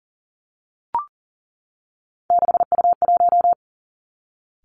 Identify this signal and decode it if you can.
Morse code, heard as a keyed tone, about 38 wpm, 690 Hz. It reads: 6U1
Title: Three Challenging Letter-Number Combinations 40wpm